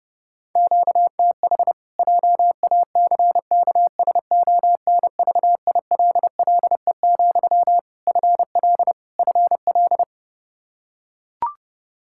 Morse code, heard as a keyed tone, about 30 wpm, 700 Hz. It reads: QTH JACKSONVILLE, FL FL
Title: Practice QSO - Exchange 1 - Beginning - 30wpm